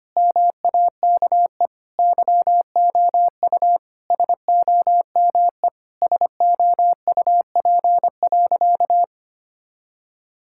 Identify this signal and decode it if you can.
Morse code, heard as a keyed tone, about 25 wpm, 700 Hz. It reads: MAKE YOU SOME SOUP.
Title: Sentences from Top 1000 Words - Review Part 6 - 25wpm